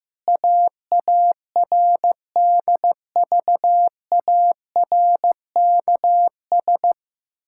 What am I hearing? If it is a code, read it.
Morse code, heard as a keyed tone, about 16 wpm, 700 Hz. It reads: AARDVARKS
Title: Pangram Challenge v2 - 15wpm